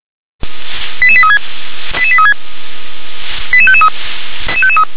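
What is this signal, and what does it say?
Five-tone selective call (ZVEI), heard as quick runs of five tones, tones of about 70 ms. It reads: 88836 88836 88553
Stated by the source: ZVEI five-tone selective call received by SDR.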